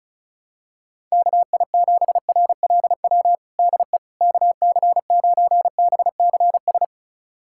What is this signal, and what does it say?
Morse code, heard as a keyed tone, about 34 wpm, 690 Hz. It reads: KI7RLW DE KC9BCS
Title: USA Callsign DE Callsign - 6 Characters - 35wpm